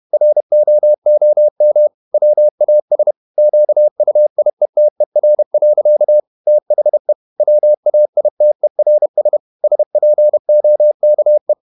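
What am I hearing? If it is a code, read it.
Morse code, heard as a keyed tone, about 31 wpm, 600 Hz. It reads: ROOM WAS QUIETER. THE WAITERS SPOKE
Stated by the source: Morse code at 30 wpm, the novel Never by Ken Follett in English, from CW med Gnister.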